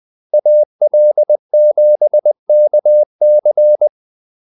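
Morse code, heard as a keyed tone, about 20 wpm, 600 Hz. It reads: AL7KC
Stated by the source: Morse call-sign practice at 20 wpm.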